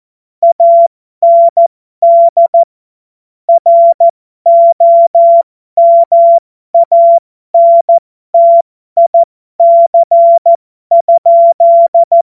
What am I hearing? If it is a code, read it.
Morse code, heard as a keyed tone, about 14 wpm, spaced as for 12 wpm, 680 Hz. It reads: AND ROMANTIC?
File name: Morsekurs / Witze